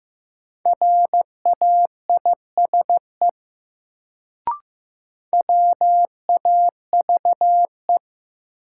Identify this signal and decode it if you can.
Morse code, heard as a keyed tone, about 15 wpm, 700 Hz. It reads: RAISE WAVE
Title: Sentences from Top 1400 Words - New Word Drill Alt - 15wpm